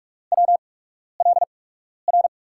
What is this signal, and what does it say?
Morse code, heard as a keyed tone, about 44 wpm, spaced as for 12 wpm, 700 Hz. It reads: WLR
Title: USA Call Signs - 5 Characters - 8x Character Spacing - 45wpm